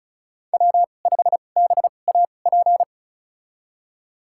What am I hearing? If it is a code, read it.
Morse code, heard as a keyed tone, about 36 wpm, spaced as for 25 wpm, 700 Hz. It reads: W5BAP
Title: USA Call Signs - 5 Characters - 2x Character Spacing - 35wpm